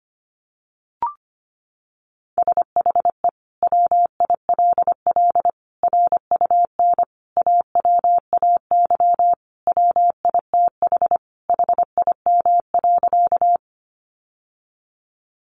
Morse code, heard as a keyed tone, about 25 wpm, 700 Hz. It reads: SHE WILL RUN AWAY WITH HIM.
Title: Sentences from Top 600 Words 25wpm